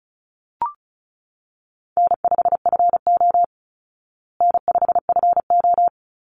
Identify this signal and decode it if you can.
Morse code, heard as a keyed tone, about 34 wpm, 700 Hz. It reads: N5FO N5FO
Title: Call Signs Repeated 3x - 35wpm